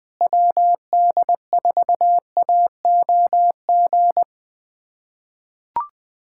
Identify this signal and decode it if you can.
Morse code, heard as a keyed tone, about 20 wpm, 700 Hz. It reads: WD4AOG
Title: USA Callsign DE Callsign 20wpm